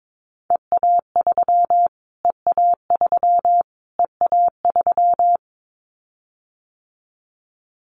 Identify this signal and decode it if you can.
Morse code, heard as a keyed tone, about 22 wpm, 700 Hz. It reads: EA3 EA3 EA3
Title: Callsign Prefixes for Top 25 DXCC - Repeated 3x - 22wpm